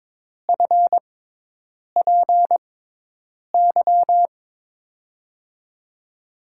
Morse code, heard as a keyed tone, about 22 wpm, spaced as for 8 wpm, 700 Hz. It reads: FPY
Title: USA Call Signs - 5 Characters - 6x Character Spacing - 22wpm